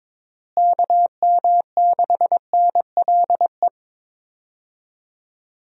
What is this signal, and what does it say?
Morse code, heard as a keyed tone, about 22 wpm, 700 Hz. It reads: KM6NLE
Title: USA Call Signs - 6 Characters - Repeated 3x - 22wpm